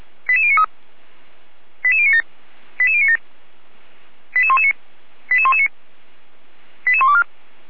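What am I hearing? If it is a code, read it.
Five-tone selective call (ZVEI), heard as quick runs of five tones, tones of about 70 ms. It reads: 88093 88097 88097 88119 88119 88124
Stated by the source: ZVEI five-tone selective call 88057, received on SDR.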